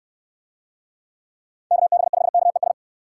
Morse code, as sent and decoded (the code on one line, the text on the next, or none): -.- -... ..... -.- . ...
KB5KES